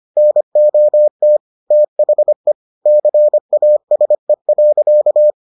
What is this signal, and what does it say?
Morse code, heard as a keyed tone, about 25 wpm, 600 Hz. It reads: NOT THE CASE.